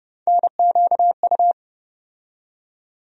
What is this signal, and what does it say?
Morse code, heard as a keyed tone, about 30 wpm, 700 Hz. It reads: NQU
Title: USA Call Sign Postfixes - 3 Letters 30wpm